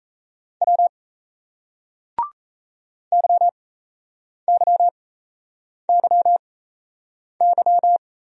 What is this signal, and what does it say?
Morse code, heard as a keyed tone, about 36 wpm, spaced as for 8 wpm, 700 Hz. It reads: W YYYY